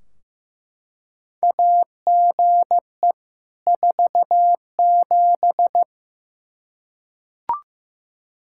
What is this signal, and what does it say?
Morse code, heard as a keyed tone, about 15 wpm, 700 Hz. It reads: AGE 47
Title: Practice QSO - Exchange 3 - More Info - 15wpm